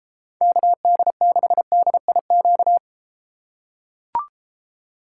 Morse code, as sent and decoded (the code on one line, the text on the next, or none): -.- -.. -.... -.. .. --.-
KD6DIQ